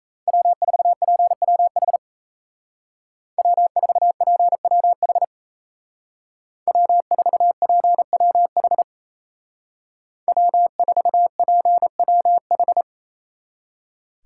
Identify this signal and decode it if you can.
Morse code, heard as a keyed tone, about 34 wpm, 700 Hz. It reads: W4PWH W4PWH W4PWH W4PWH
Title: Speed Racing - USA 1x3 Call Signs 28wpm